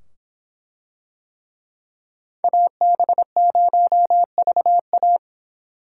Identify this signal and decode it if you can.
Morse code, heard as a keyed tone, about 24 wpm, 700 Hz. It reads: AB0VA